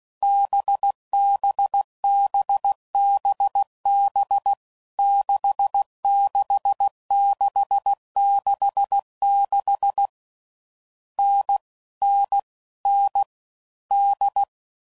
Morse code, heard as a keyed tone, about 16 wpm, 800 Hz. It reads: BBBBB 66666 N N N D
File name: Combinations / Difficult Combos / NDB NDB6